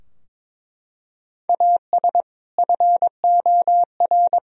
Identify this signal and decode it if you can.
Morse code, heard as a keyed tone, about 22 wpm, 700 Hz. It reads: AS FOR
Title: Sentences from Top 1000 Words - Review Part 4 - 22wpm